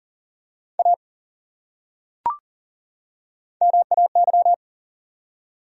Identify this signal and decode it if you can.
Morse code, heard as a keyed tone, about 40 wpm, 700 Hz. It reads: A MAY
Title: Top 100 Words 40wpm